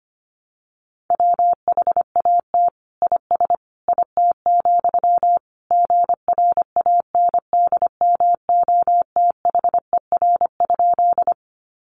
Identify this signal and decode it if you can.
Morse code, heard as a keyed tone, about 25 wpm, 700 Hz. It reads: WHAT IS IT, GRANDMOTHER?
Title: Sentences from Top 1200 Words 25wpm